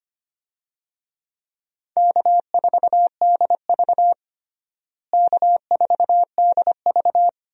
Morse code, heard as a keyed tone, about 25 wpm, 700 Hz. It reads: K4DV K4DV
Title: USA Call Signs - 4 Characters - Repeated 2x - 25wpm